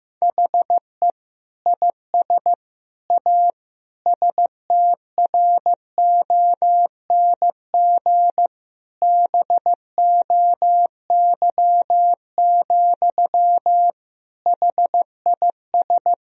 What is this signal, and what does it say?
Morse code, heard as a keyed tone, about 15 wpm, 700 Hz. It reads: HE IS A STRONG BOY, HIS
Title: Sentences from Top 1000 Words - Review Part 3 - 15wpm